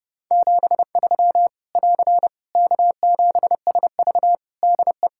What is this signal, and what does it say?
Morse code, heard as a keyed tone, about 30 wpm, 700 Hz. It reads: 73 <AR> K7SV DE
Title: Practice QSO - Exchange 4 - Ending - 30wpm